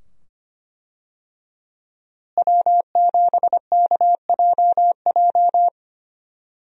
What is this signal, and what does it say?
Morse code, heard as a keyed tone, about 25 wpm, 700 Hz. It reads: W7KJJ
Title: USA Call Signs - 5 Characters - Repeated 2x - 25wpm